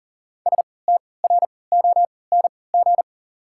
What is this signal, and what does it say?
Morse code, heard as a keyed tone, about 40 wpm, spaced as for 23 wpm, 690 Hz. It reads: STRONG
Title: 3x Character Spacing - Top 500 Words - 40wpm